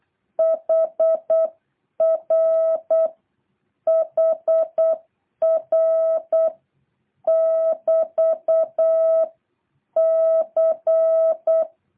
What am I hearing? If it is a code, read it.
Morse code, heard as a keyed tone, about 8 wpm, 660 Hz. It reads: HR HR <BT> C
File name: jh8cbh / 2023cw